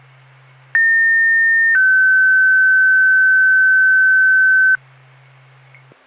Two-tone page, heard as a one-second tone, then a three-second tone, about 1800 then 1550 Hz. It is heard from 0.7 to 4.8 s.